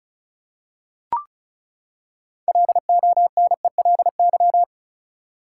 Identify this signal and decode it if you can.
Morse code, heard as a keyed tone, about 34 wpm, 700 Hz. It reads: LONELY
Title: Six Letter Words - 35wpm